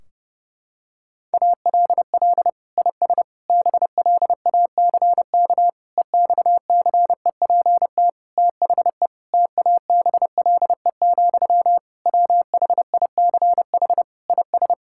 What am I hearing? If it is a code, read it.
Morse code, heard as a keyed tone, about 30 wpm, 700 Hz. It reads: ALL IS BLACK EXCEPT THE TABLE, WHICH IS
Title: Sentences from Top 1600 Words 30wpm